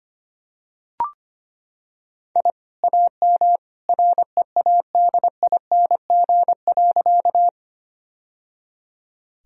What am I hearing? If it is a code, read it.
Morse code, heard as a keyed tone, about 25 wpm, 700 Hz. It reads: I AM READING.
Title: Sentences from Top 1000 Words - Review Part 3 - 25wpm